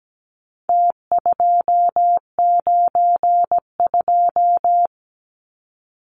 Morse code, heard as a keyed tone, about 17 wpm, 700 Hz. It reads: T292
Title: Numbers - Contest Style - 4-Digit 17wpm